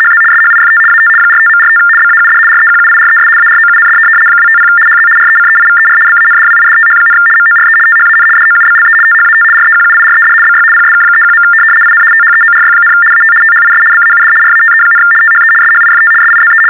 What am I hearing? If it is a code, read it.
RTTY, about 75 baud. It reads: HZ SHIFT AT 45.45 BAUD REQUIRES AROUND 250 HZ RECEIVER BANDWIDTH, MORE THAN DOUBLE THAT REQUIRED BY PSK31. IN THEORY, AT THIS BAUD RATE, THE SHIFT SIZE
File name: teletype-RTTY-75